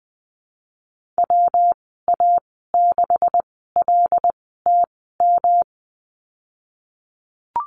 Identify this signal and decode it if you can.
Morse code, heard as a keyed tone, about 20 wpm, spaced as for 15 wpm, 700 Hz. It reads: WA6LTM
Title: 179 Lesson 29g - Slash - Call Signs 20wpm